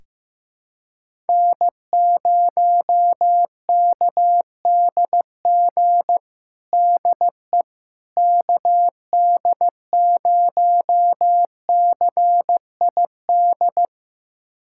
Morse code, heard as a keyed tone, about 16 wpm, 700 Hz. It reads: N0KDG DE KD0CID
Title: USA Callsign DE Callsign 15wpm